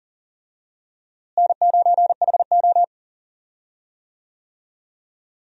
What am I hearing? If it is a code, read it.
Morse code, heard as a keyed tone, about 40 wpm, 700 Hz. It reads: N9HO
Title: USA Call Signs - 4 Characters - Repeated 2x - 40wpm